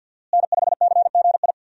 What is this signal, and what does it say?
Morse code, heard as a keyed tone, about 46 wpm, 700 Hz. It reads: N5CGI
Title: USA Call Signs - 5 Characters - Repeated 2x - 50wpm